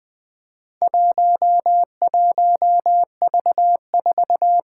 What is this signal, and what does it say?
Morse code, heard as a keyed tone, about 20 wpm, 700 Hz. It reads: 11V4
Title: Four Challenging Letter-Number Combinations 20wpm